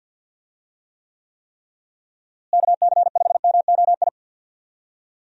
Morse code, heard as a keyed tone, about 53 wpm, 700 Hz. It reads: KC5MOI
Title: USA Call Signs - 6 Characters - Repeated 3x - 50wpm